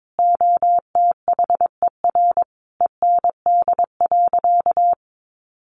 Morse code, heard as a keyed tone, about 22 wpm, 700 Hz. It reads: OTHER END.
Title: Easy Phrases 22wpm